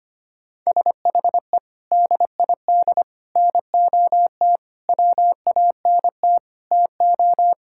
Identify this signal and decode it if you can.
Morse code, heard as a keyed tone, about 25 wpm, 700 Hz. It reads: SHE DID NOT WANT TO
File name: practice / 25wpm / Sentences with 4-Word Phrases - 25wpm